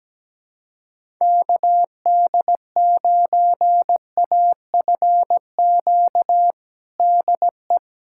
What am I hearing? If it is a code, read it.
Morse code, heard as a keyed tone, about 17 wpm, 700 Hz. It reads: KD9AFQ DE
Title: USA Callsign DE Callsign - 6 Characters - 17wpm